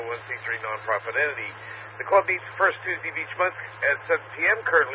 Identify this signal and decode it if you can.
Emergency Alert System (SAME) tones, heard as harsh buzzing bursts, with speech. It is heard from 1.1 to 4.0 s.